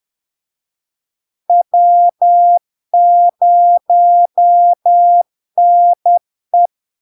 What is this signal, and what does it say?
Morse code, heard as a keyed tone, about 10 wpm, 700 Hz. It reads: W0NE
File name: LARK-FD-2022-Log-in-CW-10wpm-700-Hz